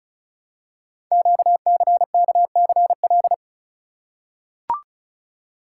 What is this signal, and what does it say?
Morse code, heard as a keyed tone, about 36 wpm, 700 Hz. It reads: QCKCL
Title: Code Groups - 5 Letters - 35wpm